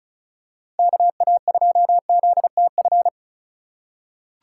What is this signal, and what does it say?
Morse code, heard as a keyed tone, about 34 wpm, 700 Hz. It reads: KA2ZTF
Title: USA Call Signs - 2x Character Spacing - 35wpm